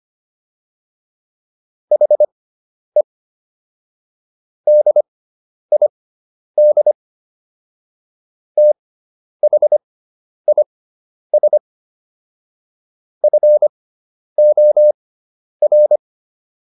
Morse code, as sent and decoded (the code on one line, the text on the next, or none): .... / -.. .. -.. / - .... .. ... / ..-. --- .-.
HE DID THIS FOR